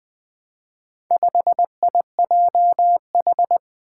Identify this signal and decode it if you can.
Morse code, heard as a keyed tone, about 20 wpm, 700 Hz. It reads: 5IJH
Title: Four Challenging Letter-Number Combinations 20wpm